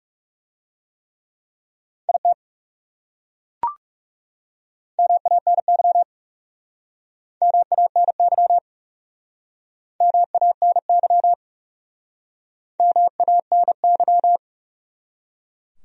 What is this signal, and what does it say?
Morse code, heard as a keyed tone, about 40 wpm, 700 Hz. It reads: IT MANY MANY MANY MANY